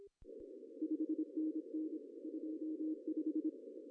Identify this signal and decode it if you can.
Morse code, heard as a keyed tone, about 26 wpm, 310 Hz. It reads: 5NN 25
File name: CW / cqww-cw-k5tr